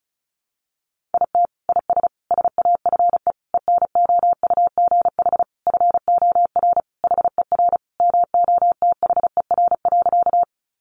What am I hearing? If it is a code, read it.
Morse code, heard as a keyed tone, about 34 wpm, 700 Hz. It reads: IT IS SAFE ENOUGH FOR HER MOTHER.
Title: Sentences from Top 900 Words 35wpm